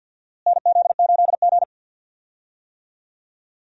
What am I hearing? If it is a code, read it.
Morse code, heard as a keyed tone, about 52 wpm, 700 Hz. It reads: NZ8G
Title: USA Call Signs - 4 Characters - Repeated 3x - 50wpm